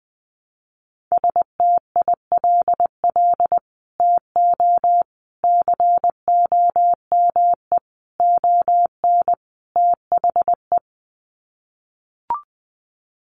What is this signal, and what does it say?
Morse code, heard as a keyed tone, about 20 wpm, 700 Hz. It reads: STILL TO COME ON THE
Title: Sets of 5 Words Encore2 20wpm